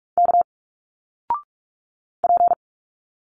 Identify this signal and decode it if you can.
Morse code, heard as a keyed tone, about 45 wpm, 700 Hz. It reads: K P
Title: Single Letters 45wpm